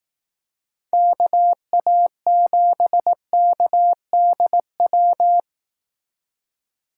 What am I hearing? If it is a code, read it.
Morse code, heard as a keyed tone, about 18 wpm, 700 Hz. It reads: KA7KDW
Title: Speed Racing - USA 2x3 Call Signs 15wpm